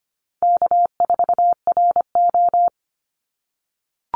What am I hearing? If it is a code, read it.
Morse code, heard as a keyed tone, about 25 wpm, 700 Hz. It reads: K4RO
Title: Call Signs - 25wpm